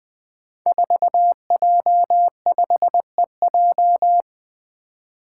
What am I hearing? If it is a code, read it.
Morse code, heard as a keyed tone, about 20 wpm, 700 Hz. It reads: 4J5EJ
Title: Five Challenging Letter-Number Combinations 20wpm